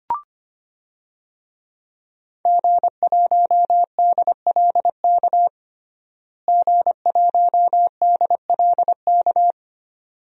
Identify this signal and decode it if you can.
Morse code, heard as a keyed tone, about 25 wpm, 700 Hz. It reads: G1DLK G1DLK
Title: DXCC-100 Call Signs - 5 Characters - Repeated 2x - 25wpm